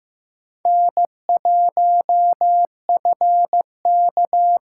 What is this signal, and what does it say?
Morse code, heard as a keyed tone, about 15 wpm, 700 Hz. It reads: N1FK
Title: USA Callsign DE Callsign - 4 Characters - 15wpm